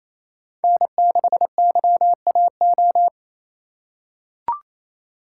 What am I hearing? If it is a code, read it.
Morse code, heard as a keyed tone, about 28 wpm, 700 Hz. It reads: N6YAO
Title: USA Call Signs - 5 Characters - 2x Character Spacing - 28wpm